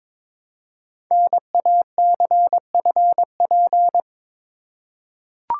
Morse code, heard as a keyed tone, about 22 wpm, 700 Hz. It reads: NACFP
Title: Code Groups - 5 Letter:Numbers - 22wpm